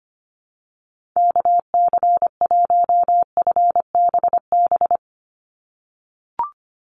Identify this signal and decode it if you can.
Morse code, heard as a keyed tone, about 25 wpm, 700 Hz. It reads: KC1FBB
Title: USA Call Signs 25wpm